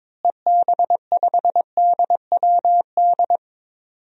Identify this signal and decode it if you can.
Morse code, heard as a keyed tone, about 21 wpm, 700 Hz. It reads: EB5DWD
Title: DXCC-100 Call Signs - 6 Characters - Repeated 3x - 22wpm